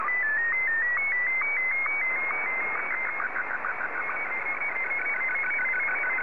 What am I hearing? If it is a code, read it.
SSTV (slow-scan television), Martin 1.